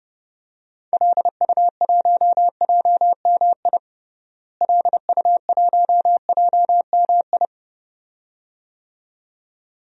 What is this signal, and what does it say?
Morse code, heard as a keyed tone, about 32 wpm, 700 Hz. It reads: LU1JMI LU1JMI
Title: DXCC-100 Call Signs - 6 Characters - Repeated 2x - 30wpm